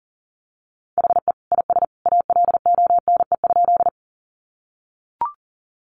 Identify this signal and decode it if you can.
Morse code, heard as a keyed tone, about 40 wpm, 700 Hz. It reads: HE IS ALONE?
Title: Sentences from Top 1000 Words 40wpm